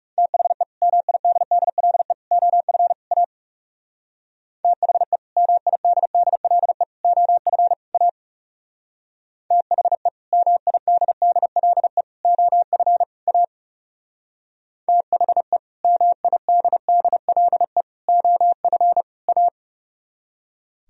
Morse code, heard as a keyed tone, about 39 wpm, 700 Hz. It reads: THE MIDDLE OF A THE MIDDLE OF A THE MIDDLE OF A THE MIDDLE OF A